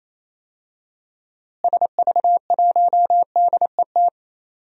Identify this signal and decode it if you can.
Morse code, heard as a keyed tone, about 28 wpm, 700 Hz. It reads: SV1DET
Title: DXCC-100 Call Signs - 6 Characters - Repeated 3x - 28wpm